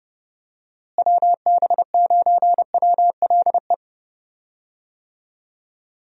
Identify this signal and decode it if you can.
Morse code, heard as a keyed tone, about 31 wpm, 700 Hz. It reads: WB9WLE
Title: USA Call Signs - 6 Characters - Repeated 3x - 30wpm